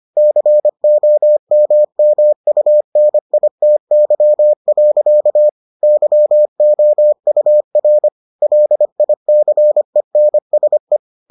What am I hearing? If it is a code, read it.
Morse code, heard as a keyed tone, about 25 wpm, 600 Hz. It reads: COMMUNITY. YOUR LICENSE